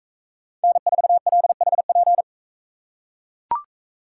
Morse code, heard as a keyed tone, about 44 wpm, 700 Hz. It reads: N4LHP